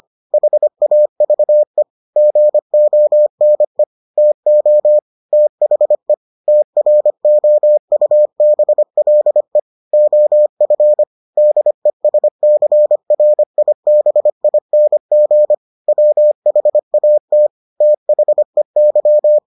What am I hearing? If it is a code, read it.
Morse code, heard as a keyed tone, about 25 wpm, 600 Hz. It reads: HAVE GONE TO THE TROUBLE OF DESCRIBING WHAT THEY